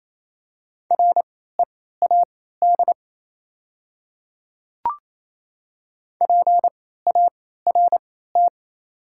Morse code, heard as a keyed tone, about 28 wpm, spaced as for 16 wpm, 700 Hz. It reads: READ PART
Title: 3x Character Spacing - Top 200 Words - 28wpm